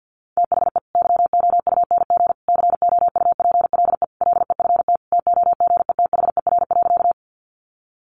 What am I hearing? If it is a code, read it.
Morse code, heard as a keyed tone, about 50 wpm, 700 Hz. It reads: THE YOUNG COUPLE LEFT TOGETHER.
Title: Sentences from Top 1100 Words 50wpm